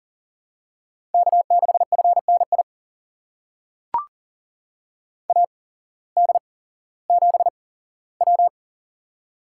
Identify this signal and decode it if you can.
Morse code, heard as a keyed tone, about 40 wpm, spaced as for 19 wpm, 700 Hz. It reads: K6FNI AD7W
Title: USA Call Signs - 8x Character Spacing - 40wpm